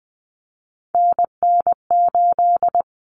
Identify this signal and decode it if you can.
Morse code, heard as a keyed tone, about 20 wpm, 700 Hz. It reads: NN8
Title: Common Cut Numbers - 3-Digit 20wpm